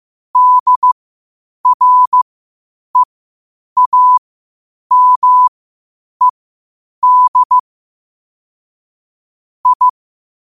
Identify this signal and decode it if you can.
Morse code, heard as a keyed tone, about 15 wpm, spaced as for 8 wpm, 1000 Hz. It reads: DREAMED I
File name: qotdfiles / 8wpm